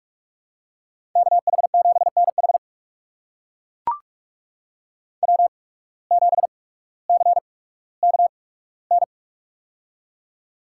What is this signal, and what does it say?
Morse code, heard as a keyed tone, about 41 wpm, spaced as for 19 wpm, 700 Hz. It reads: KH7NH W7CKN
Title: USA Call Signs - 5 Characters - 8x Character Spacing - 45wpm